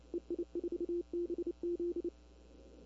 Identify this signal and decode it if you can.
Morse code, heard as a keyed tone, about 28 wpm, 350 Hz. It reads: EI4BZ
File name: CW / 2005-k5tr arrl cw dx -20:00